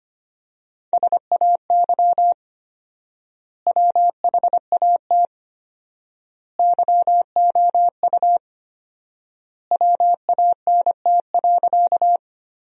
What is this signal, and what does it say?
Morse code, heard as a keyed tone, about 25 wpm, 700 Hz. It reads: SAY WHAT YOU WANT.
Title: 4x Word-Spacing - Sentences from Top 200 Words - 25wpm